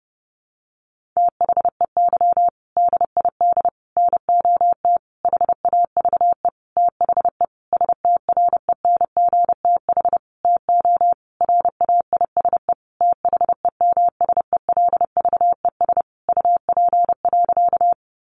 Morse code, heard as a keyed tone, about 28 wpm, 700 Hz. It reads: THEY DID NOT HAVE THE STRENGTH TO RAISE THEMSELVES UP.
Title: Sentences from Top 1400 Words 30wpm